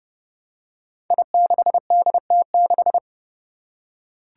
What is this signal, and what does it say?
Morse code, heard as a keyed tone, about 30 wpm, 700 Hz. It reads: I6DT6
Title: Five Challenging Letter-Number Combinations 30wpm